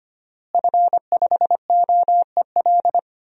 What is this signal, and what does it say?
Morse code, heard as a keyed tone, about 25 wpm, 700 Hz. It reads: F5OEL
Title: DXCC-100 Call Signs - 5 Characters - Repeated 2x - 25wpm